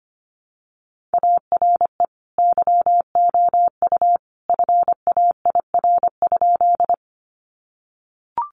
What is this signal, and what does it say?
Morse code, heard as a keyed tone, about 25 wpm, 700 Hz. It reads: ARE YOU FAIR?
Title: Sentences from Top 1300 Words 25wpm